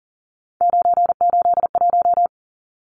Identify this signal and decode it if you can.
Morse code, heard as a keyed tone, about 40 wpm, 700 Hz. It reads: 981
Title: Numbers - 3-Digit 40wpm